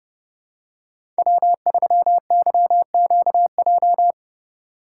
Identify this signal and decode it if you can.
Morse code, heard as a keyed tone, about 30 wpm, 700 Hz. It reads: W3YQJ
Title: Code Groups - 5 Letter:Numbers - 30wpm